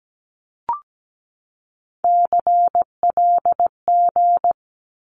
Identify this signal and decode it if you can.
Morse code, heard as a keyed tone, about 17 wpm, 700 Hz. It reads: CLG
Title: Abbreviations 17wpm